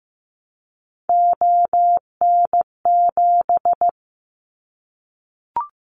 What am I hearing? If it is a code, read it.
Morse code, heard as a keyed tone, about 15 wpm, 700 Hz. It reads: ON7
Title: Callsign Prefixes for Top 50 DXCC - 15wpm